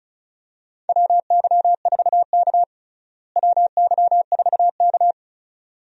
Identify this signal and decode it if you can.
Morse code, heard as a keyed tone, about 34 wpm, 700 Hz. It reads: WY4K WY4K
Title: USA Call Signs - 4 Characters - Repeated 2x - 35wpm